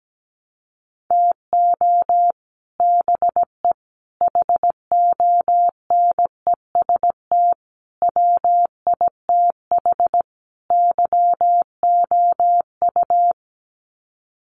Morse code, heard as a keyed tone, about 17 wpm, 700 Hz. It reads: TO BE HONEST WITH YOU
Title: Sets of 5 Words Encore 17wpm